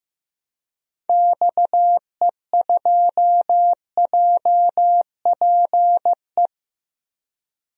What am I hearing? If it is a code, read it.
Morse code, heard as a keyed tone, about 15 wpm, 700 Hz. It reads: XE2JPE
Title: DXCC-100 Call Signs - 6 Characters - Repeated 2x - 15wpm